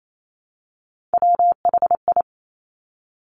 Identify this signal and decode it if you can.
Morse code, heard as a keyed tone, about 28 wpm, 700 Hz. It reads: WHI
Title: Top Three Letter Combinations in Words 28wpm